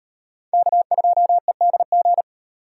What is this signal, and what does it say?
Morse code, heard as a keyed tone, about 36 wpm, 700 Hz. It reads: K2EDG